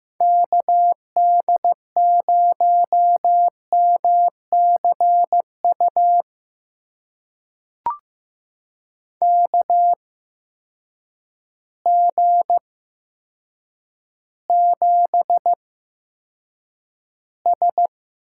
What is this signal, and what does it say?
Morse code, heard as a keyed tone, about 15 wpm, spaced as for 8 wpm, 700 Hz. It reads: KD0MCU KG7S
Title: USA Call Signs - 6 Characters - 8x Character Spacing - 15wpm